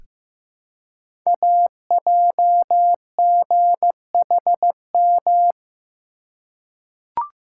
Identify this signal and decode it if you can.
Morse code, heard as a keyed tone, about 15 wpm, 700 Hz. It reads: AJGHM
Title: Code Groups - 5 Letters - 15wpm